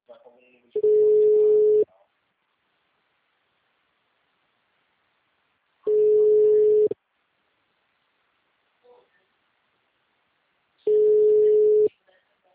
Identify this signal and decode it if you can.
Ringback tone (Europe, 425 Hz), with speech.
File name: out- -2005- - - .9539